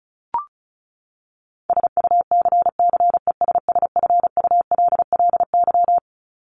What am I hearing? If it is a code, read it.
Morse code, heard as a keyed tone, about 34 wpm, 700 Hz. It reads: SUCCESSFULLY
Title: Top 1501-1750 Words 35wpm